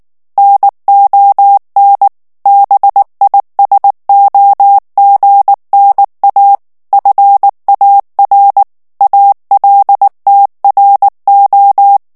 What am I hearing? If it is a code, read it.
Morse code, heard as a keyed tone, about 19 wpm, 800 Hz. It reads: NON BISOGNA FAR ALTRO
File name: morse mp3 / ITALIEN texte clair 2 vitesse 18 mots minute